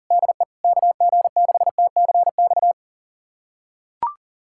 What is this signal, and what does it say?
Morse code, heard as a keyed tone, about 40 wpm, 700 Hz. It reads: DE KG6TCX
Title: USA Callsign DE Callsign 40wpm